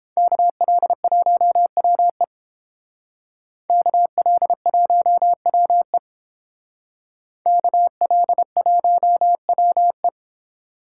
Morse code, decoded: KL1WE KL1WE KL1WE

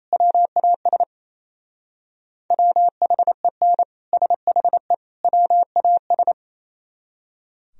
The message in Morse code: WAS WHEN SHE WAS

